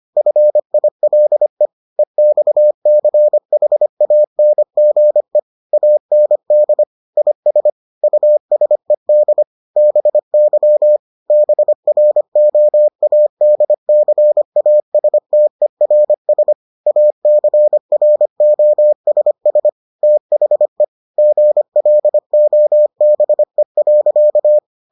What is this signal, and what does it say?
Morse code, heard as a keyed tone, about 25 wpm, 600 Hz. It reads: FILE EXCHANGE AND IS USED BY BROADCASTERS ACROSS THE GLOBE.